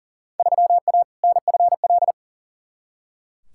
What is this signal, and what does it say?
Morse code, heard as a keyed tone, about 40 wpm, 700 Hz. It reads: 3A NFL
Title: Field Day Exchanges - 40wpm